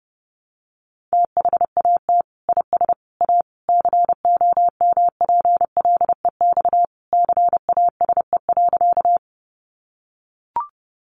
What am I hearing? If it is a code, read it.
Morse code, heard as a keyed tone, about 30 wpm, 700 Hz. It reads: THAT IS A COMPLEX CASE.
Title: Sentences from Top 900 Words 30wpm